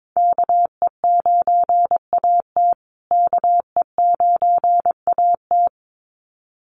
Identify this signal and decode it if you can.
Morse code, heard as a keyed tone, about 22 wpm, 700 Hz. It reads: KE9AT KE9AT
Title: USA Call Signs Repeated 2x 22wpm